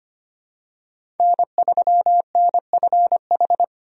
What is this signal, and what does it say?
Morse code, heard as a keyed tone, about 25 wpm, 700 Hz. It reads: N3NFH
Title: USA Call Signs - 5 Characters - 25wpm